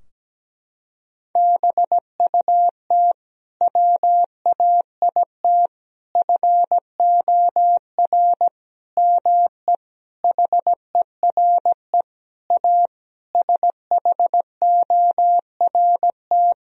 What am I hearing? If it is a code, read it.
Morse code, heard as a keyed tone, about 17 wpm, 700 Hz. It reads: BUT WAIT FOR ME HERE A SHORT
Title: Sentences from Top 1000 Words - Review Part 2 - 17wpm